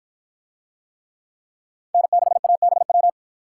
Morse code, heard as a keyed tone, about 48 wpm, 700 Hz. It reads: N6ABW